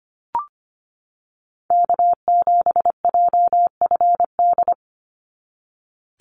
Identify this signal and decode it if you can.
Morse code, heard as a keyed tone, about 25 wpm, 700 Hz. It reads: K7JFD
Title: NA SOTA Activator Callsigns 25wpm